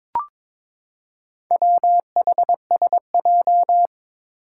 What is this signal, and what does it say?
Morse code, heard as a keyed tone, about 21 wpm, 700 Hz. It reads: WHSJ
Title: Four Challenging Letter-Number Combinations 22wpm